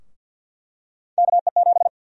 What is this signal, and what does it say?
Morse code, heard as a keyed tone, about 49 wpm, 700 Hz. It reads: KE7
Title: USA Call Sign Prefixes - Repeated 2x - 50wpm